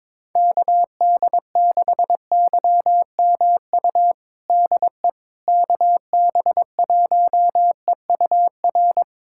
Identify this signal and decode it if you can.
Morse code, heard as a keyed tone, about 22 wpm, 700 Hz. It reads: KD6YMU DE KB1EUR
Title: USA Callsign DE Callsign - 6 Characters - 22wpm